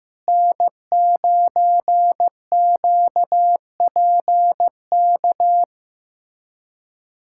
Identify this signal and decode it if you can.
Morse code, heard as a keyed tone, about 15 wpm, 700 Hz. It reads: N9QPK
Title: USA Callsign DE Callsign 15wpm